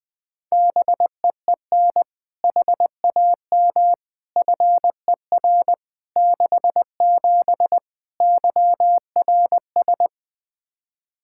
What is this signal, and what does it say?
Morse code, heard as a keyed tone, about 20 wpm, 700 Hz. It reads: BEEN HAM FER 67 YRS
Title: Practice QSO - Exchange 3 - More Info - 20wpm